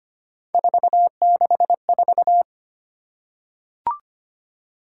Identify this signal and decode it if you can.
Morse code, heard as a keyed tone, about 25 wpm, 700 Hz. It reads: 464
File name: Three Letter-Number Combinations - 25wpm